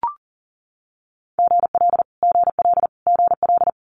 Morse code, heard as a keyed tone, about 38 wpm, 700 Hz. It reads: GL GL GL
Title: Multi-letter Phonemes Repeated 3x 40wpm